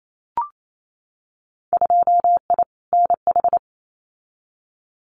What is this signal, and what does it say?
Morse code, heard as a keyed tone, about 28 wpm, 700 Hz. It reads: 2I NH
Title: Winter Field Day Exchanges Encore 28wpm